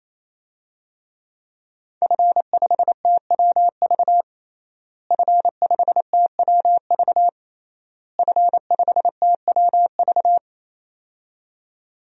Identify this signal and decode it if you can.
Morse code, heard as a keyed tone, about 28 wpm, 700 Hz. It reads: F5TWV F5TWV F5TWV
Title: DXCC-100 Call Signs - 5 Characters - Repeated 3x - 28wpm